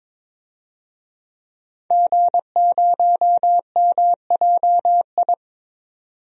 Morse code, heard as a keyed tone, about 22 wpm, 700 Hz. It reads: G0MJI
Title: DXCC-100 Call Signs - 5 Characters - 22wpm